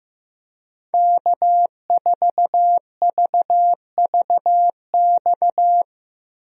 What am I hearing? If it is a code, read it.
Morse code, heard as a keyed tone, about 15 wpm, 700 Hz. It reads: K4VVX
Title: USA Call Signs - 5 Characters - Repeated 3x - 15wpm